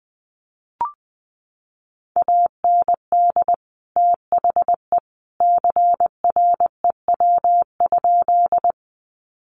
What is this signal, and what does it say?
Morse code, heard as a keyed tone, about 20 wpm, 700 Hz. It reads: AND THE CREW?
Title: Sentences from Top 1500 Words 20wpm